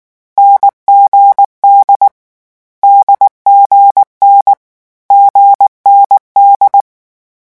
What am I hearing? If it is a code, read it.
Morse code, heard as a keyed tone, about 19 wpm, 800 Hz. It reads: NGD DGN GND
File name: morse mp3 / lecon4-vitesse 18